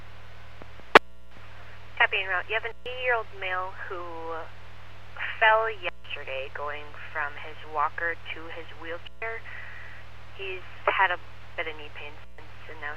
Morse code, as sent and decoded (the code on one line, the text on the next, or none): none